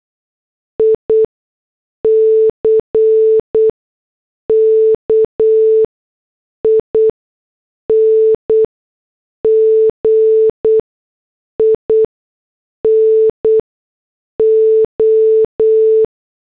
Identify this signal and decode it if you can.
Morse code, heard as a keyed tone, about 8 wpm, spaced as for 6 wpm, 440 Hz. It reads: ICKING IN O